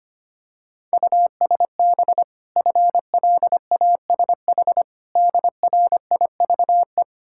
Morse code, heard as a keyed tone, about 25 wpm, 700 Hz. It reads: USB FLASH DRIVE
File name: Easy Words - Speed-Racing with Hints - 25wpm